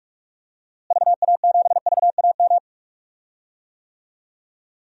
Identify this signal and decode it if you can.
Morse code, heard as a keyed tone, about 46 wpm, 700 Hz. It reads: VA7VAM
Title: DXCC-100 Call Signs - 6 Characters - Repeated 2x - 45wpm